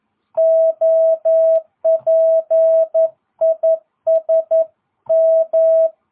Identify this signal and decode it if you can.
Morse code, heard as a keyed tone, about 11 wpm, 650 Hz. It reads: OP IS M